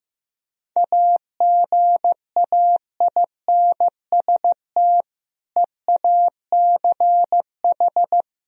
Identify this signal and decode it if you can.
Morse code, heard as a keyed tone, about 15 wpm, 700 Hz. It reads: AGAINST EACH
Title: Sentences from Top 1000 Words - Review Part 1 - 15wpm